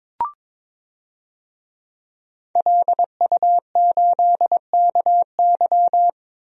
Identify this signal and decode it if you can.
Morse code, heard as a keyed tone, about 22 wpm, 700 Hz. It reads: LU8KY
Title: DXCC-100 Call Signs - 5 Characters - 22wpm